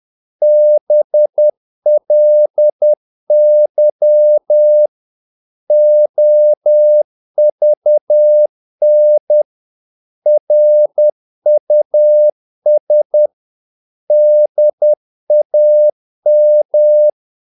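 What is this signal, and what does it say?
Morse code, heard as a keyed tone, about 10 wpm, 600 Hz. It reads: BLY OVN RUS DAM
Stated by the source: Morse practice of short Danish words at 10 wpm, from CW med Gnister.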